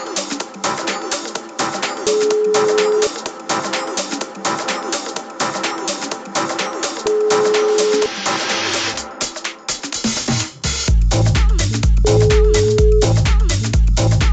Ringback tone (Europe, 425 Hz), with noise.